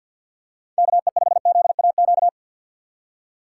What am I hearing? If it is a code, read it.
Morse code, heard as a keyed tone, about 53 wpm, 700 Hz. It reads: KE5ZAQ
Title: USA Call Signs - 6x Character Spacing - 50wpm